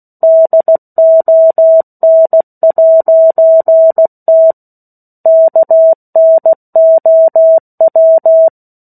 Morse code, heard as a keyed tone, about 16 wpm, 650 Hz. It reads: DON'T KNOW
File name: 03-Chapter-03 16 WPM